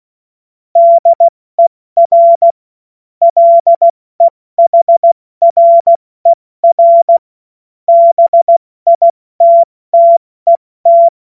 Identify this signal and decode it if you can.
Morse code, heard as a keyed tone, about 16 wpm, spaced as for 14 wpm, 680 Hz. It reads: DER LEHRER BITTET
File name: Morsekurs / Witze / Witz 13. September 2022 70 F80